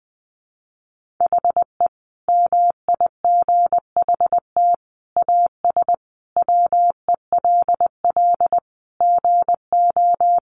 Morse code, heard as a keyed tone, about 20 wpm, 700 Hz. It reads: HE MIGHT AS WELL GO
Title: Sentences from Top 1300 Words 20wpm